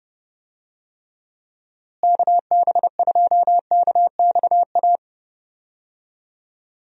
Morse code, heard as a keyed tone, about 30 wpm, 700 Hz. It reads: KB2KXA
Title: USA Call Signs - 6 Characters - Repeated 3x - 30wpm